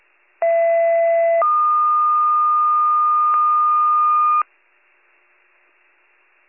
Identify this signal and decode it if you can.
Two-tone page, heard as a one-second tone, then a three-second tone, about 670 then 1220 Hz. It is heard from 0.4 to 4.4 s.